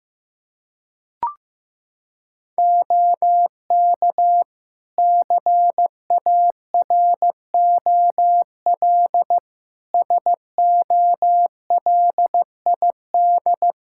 Morse code, heard as a keyed tone, about 15 wpm, 700 Hz. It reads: OK CAROL SOLID
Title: K1USN SST QSO Practice 15wpm